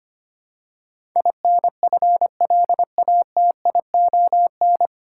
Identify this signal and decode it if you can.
Morse code, heard as a keyed tone, about 25 wpm, 700 Hz. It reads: INFLATION
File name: Nine Letter Words - 25wpm